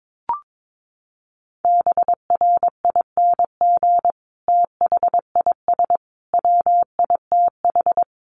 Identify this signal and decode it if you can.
Morse code, heard as a keyed tone, about 22 wpm, 700 Hz. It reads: BRING THIS WITH
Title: Sentences from Top 500 Words 22wpm